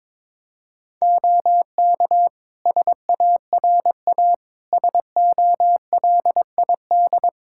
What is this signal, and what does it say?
Morse code, heard as a keyed tone, about 22 wpm, 700 Hz. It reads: OK SARA SOLID
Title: Practice QSO - Exchange 3 - More Info - 22wpm